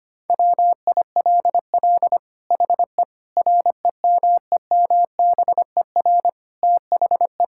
Morse code, heard as a keyed tone, about 25 wpm, 700 Hz. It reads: WILL HE REMEMBER THE
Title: Sentences from Top 1000 Words - Review Part 6 - 25wpm